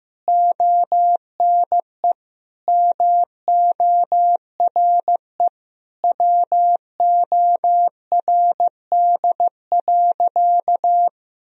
Morse code, heard as a keyed tone, about 15 wpm, 700 Hz. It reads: ONE MORE WORD.
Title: Sentences from Top 1000 Words - Review Part 1 - 15wpm